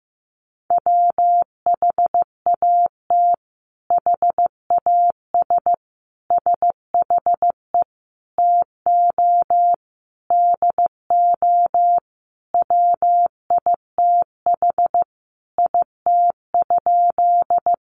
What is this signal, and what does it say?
Morse code, heard as a keyed tone, about 15 wpm, 700 Hz. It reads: WHAT HAS SHE TO DO WITH IT?
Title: Sentences from Top 100 Words 15wpm